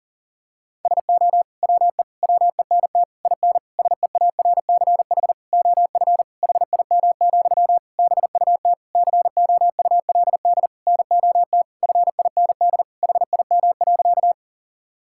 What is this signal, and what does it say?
Morse code, heard as a keyed tone, about 40 wpm, 700 Hz. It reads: SO WE WENT IN SEARCH OF HIM, BUT COULD NOT FIND HIM.
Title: Sentences from Top 1000 Words - Review Part 5 - 40wpm